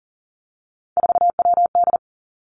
Morse code, heard as a keyed tone, about 38 wpm, 700 Hz. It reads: 4WD
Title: Three Challenging Letter-Number Combinations 40wpm